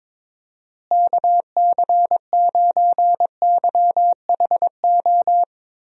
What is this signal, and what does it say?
Morse code, heard as a keyed tone, about 22 wpm, 700 Hz. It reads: KC9YHO
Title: USA Call Signs - 6 Characters - Repeated 2x - 22wpm